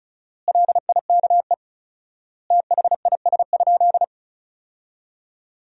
Morse code, heard as a keyed tone, about 34 wpm, 700 Hz. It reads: LIKE THIS?
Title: 4x Word-Spacing - Sentences from Top 200 Words - 35wpm